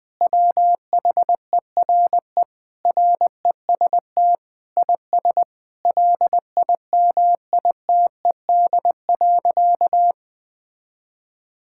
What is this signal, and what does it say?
Morse code, heard as a keyed tone, about 20 wpm, 700 Hz. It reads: WHERE REST IS LIMITED.
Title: Sentences from Top 1000 Words - Review Part 6 - 20wpm